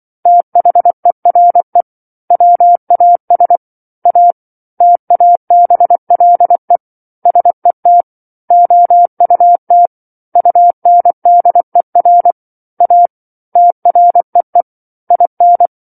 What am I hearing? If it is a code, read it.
Morse code, heard as a keyed tone, about 24 wpm, 700 Hz. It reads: THERE WAS A TABLE SET OUT UNDER A TREE IN
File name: 07-Chapter-07 24 WPM